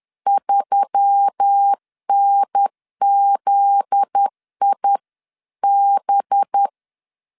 Morse code, heard as a keyed tone, about 10 wpm, 800 Hz. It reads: 3NZIB
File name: LEZIONI / 7 GF9Z / 7 GF9Z 10wpm 2611